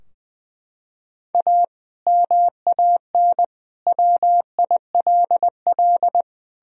Morse code, heard as a keyed tone, about 20 wpm, 700 Hz. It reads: A MAN WILL
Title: Sentences from Top 1000 Words - Review Part 2 - 20wpm